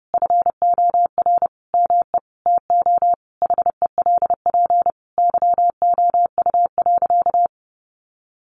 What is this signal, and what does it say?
Morse code, heard as a keyed tone, about 30 wpm, 700 Hz. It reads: FOR ME TO HELP YOU.
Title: Sentences from Top 1200 Words 30wpm